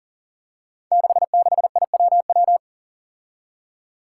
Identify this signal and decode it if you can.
Morse code, heard as a keyed tone, about 41 wpm, 700 Hz. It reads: 66IWW
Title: Five Challenging Letter-Number Combinations 40wpm